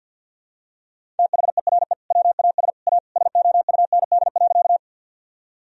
Morse code, heard as a keyed tone, about 52 wpm, 700 Hz. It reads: THERE WAS A SOUND.
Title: Sentences from Top 1000 Words - Review Part 1 - 50wpm